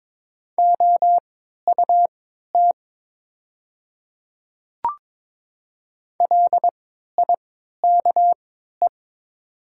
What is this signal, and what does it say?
Morse code, heard as a keyed tone, about 22 wpm, spaced as for 12 wpm, 700 Hz. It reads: OUT LIKE
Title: 3x Character Spacing - Top 100 Words - 22wpm